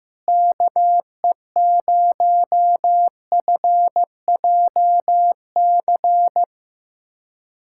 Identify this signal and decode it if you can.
Morse code, heard as a keyed tone, about 15 wpm, 700 Hz. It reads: KE0FJC
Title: USA Callsign DE Callsign - 6 Characters - 15wpm